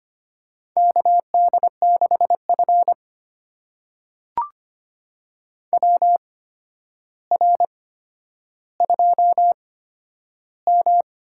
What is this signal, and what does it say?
Morse code, heard as a keyed tone, about 25 wpm, spaced as for 11 wpm, 700 Hz. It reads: KD6F WR2M
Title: USA Call Signs - 4 Characters - 8x Character Spacing - 25wpm